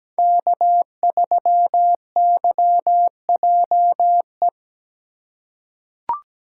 Morse code, heard as a keyed tone, about 17 wpm, 700 Hz. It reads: K3YJE
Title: USA Call Signs - 5 Characters - 3x Character Spacing - 17wpm